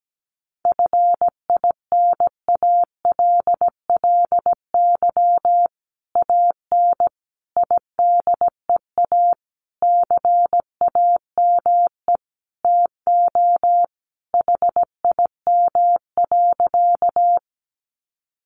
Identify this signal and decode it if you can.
Morse code, heard as a keyed tone, about 17 wpm, 700 Hz. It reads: FINALLY AN IDEA CAME TO HIM.
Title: Sentences from Top 1000 Words 17wpm